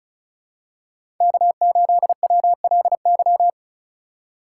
Morse code, heard as a keyed tone, about 36 wpm, 700 Hz. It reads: K8WLY
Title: USA Call Signs - 5 Characters - 4x Character Spacing - 35wpm